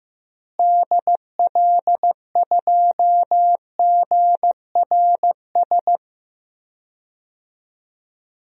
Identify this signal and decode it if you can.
Morse code, heard as a keyed tone, about 15 wpm, 700 Hz. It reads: DL2GRS
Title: DXCC-100 Call Signs - 6 Characters - Repeated 3x - 15wpm